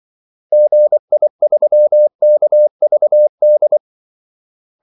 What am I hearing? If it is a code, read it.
Morse code, heard as a keyed tone, about 24 wpm, 600 Hz. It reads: GI3KVD
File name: Callsigns-24wpm